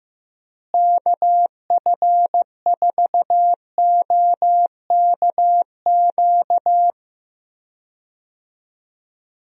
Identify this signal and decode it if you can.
Morse code, heard as a keyed tone, about 15 wpm, 700 Hz. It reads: KF4OKQ
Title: USA Call Signs - 6 Characters - Repeated 3x - 15wpm